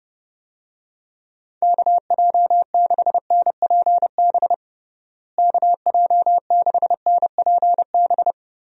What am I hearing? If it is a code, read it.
Morse code, heard as a keyed tone, about 32 wpm, 700 Hz. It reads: KJ6NPB KJ6NPB
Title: USA Call Signs - 6 Characters - Repeated 3x - 30wpm